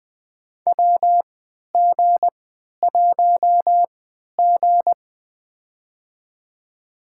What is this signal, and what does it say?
Morse code, heard as a keyed tone, about 20 wpm, spaced as for 11 wpm, 700 Hz. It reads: WG1G
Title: USA Call Signs - 3x Character Spacing - 20wpm